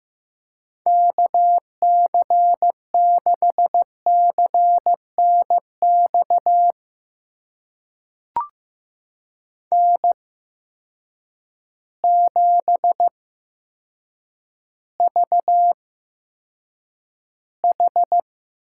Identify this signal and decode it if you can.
Morse code, heard as a keyed tone, about 15 wpm, spaced as for 7 wpm, 700 Hz. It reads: KC6CNX N7VH
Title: USA Call Signs - 8x Character Spacing - 15wpm